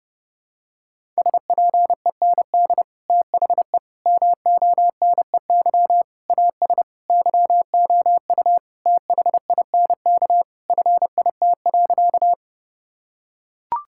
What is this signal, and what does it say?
Morse code, heard as a keyed tone, about 32 wpm, 700 Hz. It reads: SPEND THE MONEY AS YOU THINK FIT.
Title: Sentences from Top 1000 Words - Review Part 5 - 30wpm